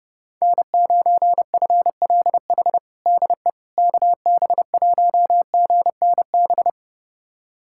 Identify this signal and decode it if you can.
Morse code, heard as a keyed tone, about 30 wpm, 700 Hz. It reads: N9FLH DE KB1GNB
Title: USA Callsign DE Callsign 30wpm